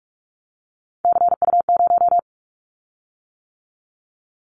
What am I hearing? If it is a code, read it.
Morse code, heard as a keyed tone, about 46 wpm, 700 Hz. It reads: CU0
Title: Callsign Prefixes for Top 50 DXCC - 45wpm